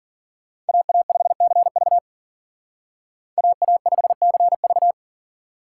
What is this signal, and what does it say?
Morse code, heard as a keyed tone, about 52 wpm, 700 Hz. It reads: AA5CV AA5CV